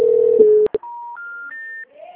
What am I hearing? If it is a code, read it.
Special information tone.
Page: Dracena (Ura-Reversa)